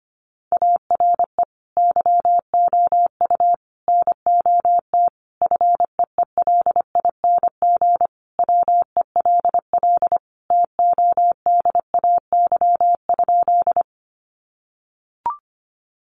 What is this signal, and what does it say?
Morse code, heard as a keyed tone, about 25 wpm, 700 Hz. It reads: ARE YOU NOT FEELING WELL TODAY?
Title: Sentences from Top 1200 Words 25wpm